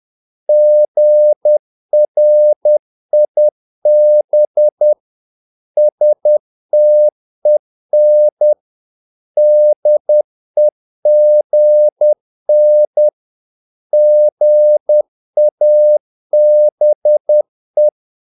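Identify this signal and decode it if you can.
Morse code, heard as a keyed tone, about 10 wpm, 600 Hz. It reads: GRIB STEN DEGN GABE
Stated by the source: Morse practice of short Danish words at 10 wpm, from CW med Gnister.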